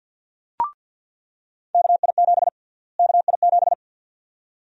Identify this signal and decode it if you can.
Morse code, heard as a keyed tone, about 47 wpm, 700 Hz. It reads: KI7 KI7
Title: USA Call Sign Prefixes - Repeated 2x - 50wpm